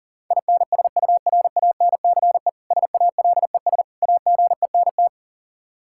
Morse code, heard as a keyed tone, about 37 wpm, 700 Hz. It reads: INSURANCE SALES AGENT
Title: Practice QSO - Exchange 3 - More Info - 40wpm